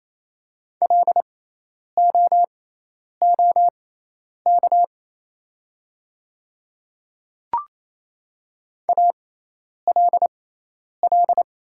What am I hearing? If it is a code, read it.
Morse code, heard as a keyed tone, about 27 wpm, spaced as for 10 wpm, 700 Hz. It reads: LOOK ALL
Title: 6x Character Spacing - Top 100 Words - 28wpm